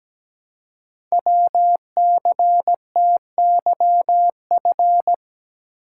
Morse code, heard as a keyed tone, about 17 wpm, 700 Hz. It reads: WCTYF
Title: Code Groups - 5 Letter:Numbers - 17wpm